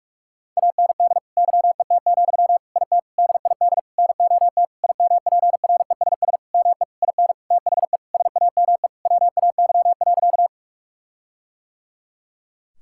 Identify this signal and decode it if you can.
Morse code, heard as a keyed tone, about 41 wpm, 700 Hz. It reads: AND YET, IT DID NOT IMPRESS ME IN THE SAME WAY.
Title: Sentences from Top 1600 Words 45wpm